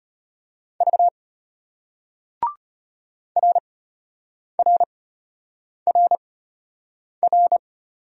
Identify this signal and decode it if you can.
Morse code, heard as a keyed tone, about 33 wpm, spaced as for 8 wpm, 700 Hz. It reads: V RRRR